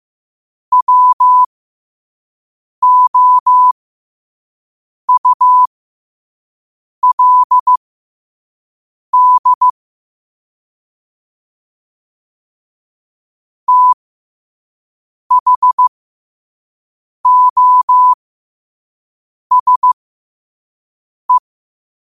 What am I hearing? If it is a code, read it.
Morse code, heard as a keyed tone, about 15 wpm, spaced as for 5 wpm, 1000 Hz. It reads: WOULD THOSE
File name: qotdfiles / 5wpm